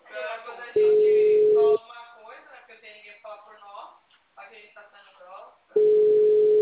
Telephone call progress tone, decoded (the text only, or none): ringback tone